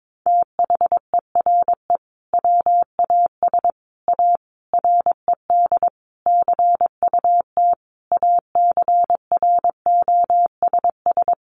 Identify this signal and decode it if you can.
Morse code, heard as a keyed tone, about 22 wpm, 700 Hz. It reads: THERE WAS A RED CUT ACROSS
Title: Sentences from Top 700 Words 22wpm